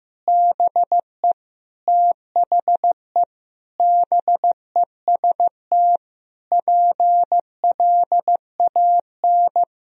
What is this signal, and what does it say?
Morse code, heard as a keyed tone, about 15 wpm, 700 Hz. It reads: BE THE BEST PLAN
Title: Sentences from Top 1000 Words - Review Part 2 - 15wpm